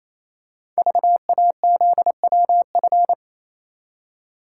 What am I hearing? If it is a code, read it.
Morse code, heard as a keyed tone, about 28 wpm, 700 Hz. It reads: VAZWF
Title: Code Groups - 5 Letter:Numbers - 28wpm